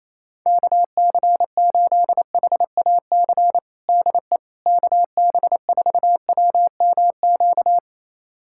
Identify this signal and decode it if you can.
Morse code, heard as a keyed tone, about 28 wpm, 700 Hz. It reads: KC8HAC DE KB4WMQ
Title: USA Callsign DE Callsign - 6 Characters - 28wpm